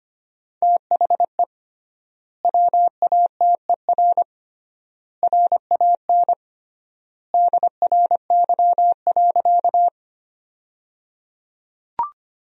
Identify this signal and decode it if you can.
Morse code, heard as a keyed tone, about 25 wpm, 700 Hz. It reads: THE WATER RAN DRY.
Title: 3x Word-Spacing - Sentences from Top 500 Words - 25wpm